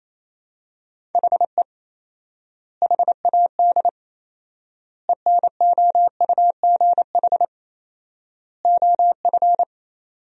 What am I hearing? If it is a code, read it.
Morse code, heard as a keyed tone, about 28 wpm, 700 Hz. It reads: HE HAD ENOUGH OF
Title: 4x Word-Spacing - Sentences from Top 400 Words - 28wpm